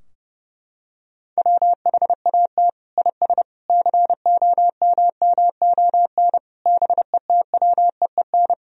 Morse code, heard as a keyed tone, about 30 wpm, 700 Hz. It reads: WHAT IS COMMON BETWEEN
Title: Sentences from Top 1000 Words - Review Part 3 - 30wpm